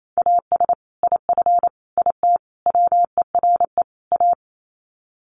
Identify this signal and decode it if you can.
Morse code, heard as a keyed tone, about 28 wpm, 700 Hz. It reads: AS IF IT WERE A
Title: Sets of 5 Words 28wpm